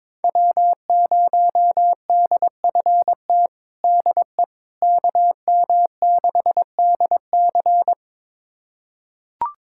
Morse code, heard as a keyed tone, about 22 wpm, 700 Hz. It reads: W0DFT DE KM6DC
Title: USA Callsign DE Callsign - 5 Characters - 22wpm